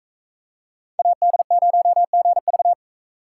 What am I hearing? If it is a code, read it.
Morse code, heard as a keyed tone, about 38 wpm, 700 Hz. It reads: AD0GV